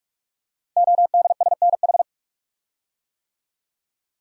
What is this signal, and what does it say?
Morse code, heard as a keyed tone, about 44 wpm, 700 Hz. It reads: ODSNH
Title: Code Groups - 5 Letter:Numbers - 45wpm